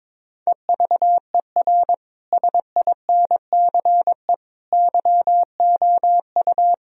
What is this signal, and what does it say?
Morse code, heard as a keyed tone, about 22 wpm, 700 Hz. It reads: EVER SINCE YOU
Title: Sentences from Top 1000 Words - Review Part 1 - 22wpm